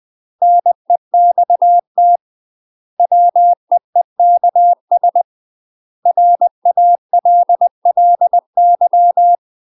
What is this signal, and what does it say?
Morse code, heard as a keyed tone, about 20 wpm, 700 Hz. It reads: NEXT WEEKS RALLY
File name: UK MASTER QSOs TRAINING / 20WPM / UK MASTER QSOs 0001